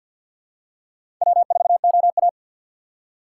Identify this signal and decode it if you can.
Morse code, heard as a keyed tone, about 49 wpm, 700 Hz. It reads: W4OA
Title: USA Call Signs - 4 Characters - 2x Character Spacing - 50wpm